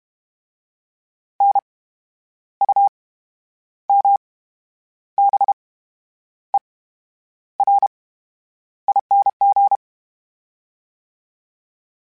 Morse code, heard as a keyed tone, about 32 wpm, 800 Hz. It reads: N U M B E R ING
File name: Special INGs / Special INGs-mp3s / VerySlowToFaster Ing @30